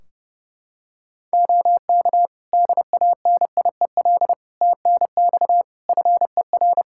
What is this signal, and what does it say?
Morse code, heard as a keyed tone, about 32 wpm, 700 Hz. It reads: OK DANIEL TNX FER
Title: K1USN SST QSO Practice 30wpm